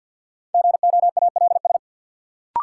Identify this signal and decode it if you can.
Morse code, heard as a keyed tone, about 53 wpm, 700 Hz. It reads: GOALS
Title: Sentences from Top 1000 Words - Review Part 6 - New Word Drill - 50wpm